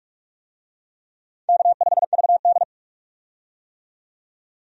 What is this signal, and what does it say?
Morse code, heard as a keyed tone, about 47 wpm, 700 Hz. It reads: K5VD
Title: USA Call Signs - 4 Characters - 45wpm